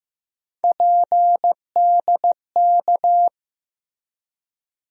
Morse code, heard as a keyed tone, about 15 wpm, 700 Hz. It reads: PDK
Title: USA Call Sign Postfixes - 3 Letters 15wpm